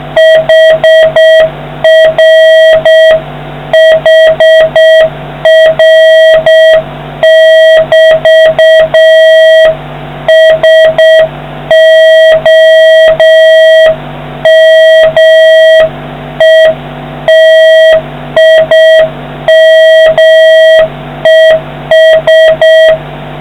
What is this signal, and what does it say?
Morse code, heard as a keyed tone, about 6 wpm, 1900 Hz, with noise. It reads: HR HR<BT> SOMETIMES